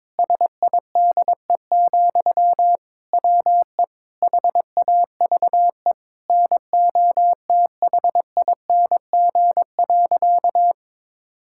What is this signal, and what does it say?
Morse code, heard as a keyed tone, about 22 wpm, 700 Hz. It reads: SIDE, WE HAVE NOTHING.
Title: Sentences from Top 1000 Words - Review Part 5 - 22wpm